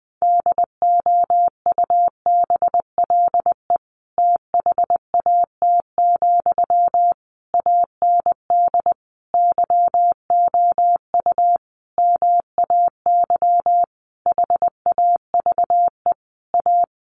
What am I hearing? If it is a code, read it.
Morse code, heard as a keyed tone, about 20 wpm, 700 Hz. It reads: DOUBLE THAT, AND YOU MAY HAVE A
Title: Sentences from Top 1400 Words 20wpm